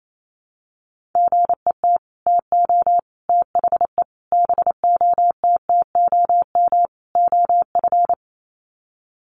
Morse code, heard as a keyed tone, about 28 wpm, 700 Hz. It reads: GET TO THE BOTTOM OF
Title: Sets of 5 Words Encore 28wpm